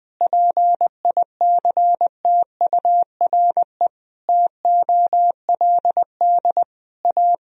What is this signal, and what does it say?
Morse code, heard as a keyed tone, about 20 wpm, 700 Hz. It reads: PICTURE TOLD A
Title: Sentences from Top 1000 Words - Review Part 1 - 20wpm